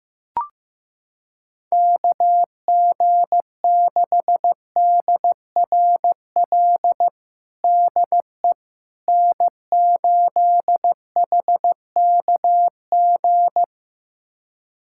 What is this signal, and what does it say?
Morse code, heard as a keyed tone, about 15 wpm, 700 Hz. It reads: KG6DRL DE N8HKG
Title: USA Callsign DE Callsign 15wpm